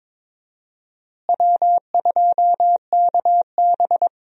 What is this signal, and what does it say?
Morse code, heard as a keyed tone, about 22 wpm, 700 Hz. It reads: W2KB
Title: USA Call Signs - 4 Characters - Repeated 2x - 22wpm